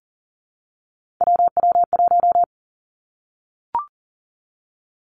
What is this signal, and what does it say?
Morse code, heard as a keyed tone, about 41 wpm, 700 Hz. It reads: WW1
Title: Three Challenging Letter-Number Combinations 40wpm